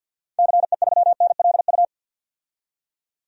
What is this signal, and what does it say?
Morse code, heard as a keyed tone, about 46 wpm, 700 Hz. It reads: CE3NLU
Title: DXCC-100 Call Signs - 6 Characters - 50wpm